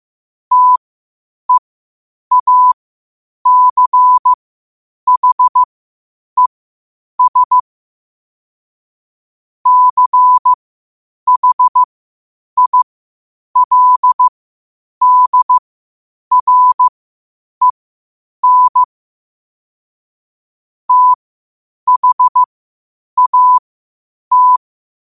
Morse code, heard as a keyed tone, about 15 wpm, spaced as for 8 wpm, 1000 Hz. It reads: TEACHES CHILDREN THAT
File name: qotdfiles / 8wpm